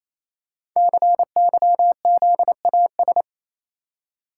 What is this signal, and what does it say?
Morse code, heard as a keyed tone, about 29 wpm, 700 Hz. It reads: CYZAS
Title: Code Groups - 5 Letters - 28wpm